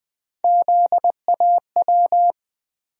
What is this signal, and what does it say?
Morse code, heard as a keyed tone, about 21 wpm, 700 Hz. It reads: ZAW
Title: USA Call Sign Postfixes - 3 Letters 20wpm